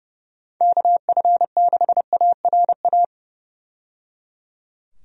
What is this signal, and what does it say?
Morse code, heard as a keyed tone, about 30 wpm, 700 Hz. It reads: KF6ARA
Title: USA Call Signs - 6 Characters - 30wpm